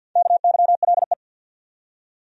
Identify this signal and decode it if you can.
Morse code, heard as a keyed tone, about 51 wpm, 700 Hz. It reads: KYLE
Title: USA Common Names - Max 5 Letters - 50wpm